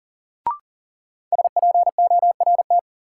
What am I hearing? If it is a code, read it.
Morse code, heard as a keyed tone, about 37 wpm, 700 Hz. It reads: SPORT
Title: Sentences from Top 1500 Words - New Word Drill Alt - 40wpm